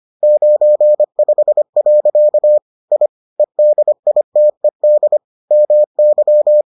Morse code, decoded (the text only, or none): 95. I EDITED MY